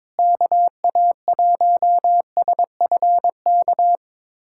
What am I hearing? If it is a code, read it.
Morse code, heard as a keyed tone, about 22 wpm, 700 Hz. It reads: KA1SFK
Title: USA Callsign DE Callsign 22wpm